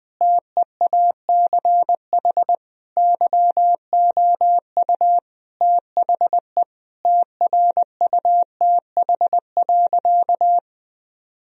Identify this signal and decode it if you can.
Morse code, heard as a keyed tone, about 20 wpm, 700 Hz. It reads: TEACH YOU THE TRUTH.
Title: Sentences from Top 1000 Words - Review Part 2 - 20wpm